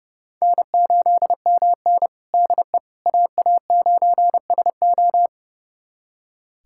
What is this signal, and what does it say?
Morse code, heard as a keyed tone, about 30 wpm, 700 Hz. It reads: N8MN DE AA9SO
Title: USA Callsign DE Callsign 30wpm